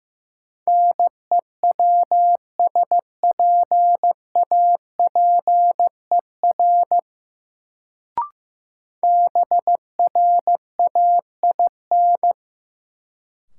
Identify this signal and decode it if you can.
Morse code, heard as a keyed tone, about 15 wpm, 700 Hz. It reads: NEWSPAPER BRAIN
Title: Sentences from Top 1100 Words - New Word Drill Alt - 15wpm